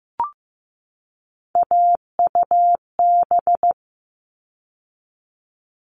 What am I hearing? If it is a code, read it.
Morse code, heard as a keyed tone, about 15 wpm, 700 Hz. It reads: AUB
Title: All Cut Numbers - 3-Digit 15wpm